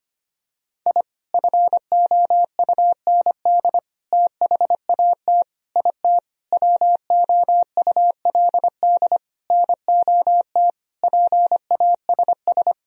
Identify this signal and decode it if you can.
Morse code, heard as a keyed tone, about 25 wpm, 700 Hz. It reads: I FOUND THAT IT WOULD NOT PASS
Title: Sentences from Top 1000 Words - Review Part 1 - 25wpm